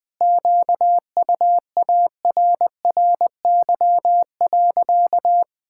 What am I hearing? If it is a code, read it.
Morse code, heard as a keyed tone, about 20 wpm, 700 Hz. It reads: QUARRY.
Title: Pangram Challenge v2 - 20wpm